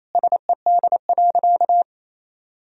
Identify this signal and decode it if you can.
Morse code, heard as a keyed tone, about 28 wpm, 700 Hz. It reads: SED.
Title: Sentences from Top 200 Words and Abbreviations 28wpm